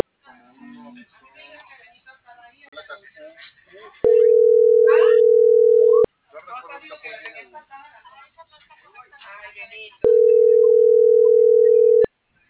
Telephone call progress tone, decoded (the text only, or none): ringback tone